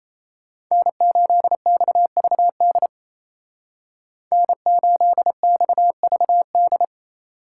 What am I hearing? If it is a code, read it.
Morse code, heard as a keyed tone, about 34 wpm, 700 Hz. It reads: N8XVD N8XVD
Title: Speed Racing - USA 1x3 Call Signs 28wpm